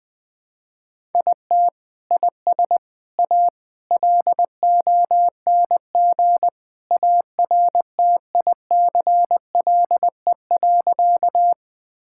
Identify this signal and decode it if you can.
Morse code, heard as a keyed tone, about 20 wpm, 700 Hz. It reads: IT IS A LONG ARTICLE.
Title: Sentences from Top 1000 Words - Review Part 5 - 20wpm